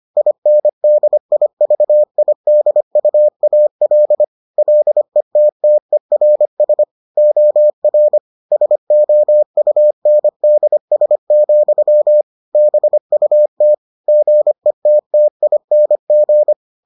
Morse code, heard as a keyed tone, about 25 wpm, 600 Hz. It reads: INDIVIDUAL LETTERS OR SOUNDS, BUT GETTING